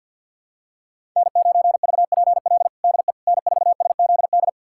Morse code, heard as a keyed tone, about 53 wpm, 700 Hz. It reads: N9VPL DE N4SZD
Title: USA Callsign DE Callsign - 5 Characters - 50wpm